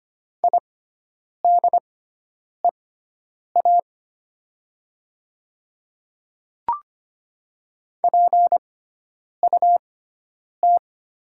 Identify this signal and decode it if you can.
Morse code, heard as a keyed tone, about 25 wpm, spaced as for 9 wpm, 700 Hz. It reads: IDEA PUT
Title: 6x Character Spacing - Top 300 Words - 25wpm